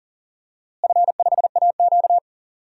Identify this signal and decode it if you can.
Morse code, heard as a keyed tone, about 40 wpm, 700 Hz. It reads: F5AQ
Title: DXCC-100 Call Signs - 4 Characters - Repeated 3x - 40wpm